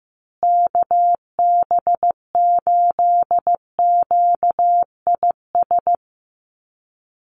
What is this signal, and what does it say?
Morse code, heard as a keyed tone, about 15 wpm, 700 Hz. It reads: KB8QIS
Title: USA Call Signs Repeated 3x 15wpm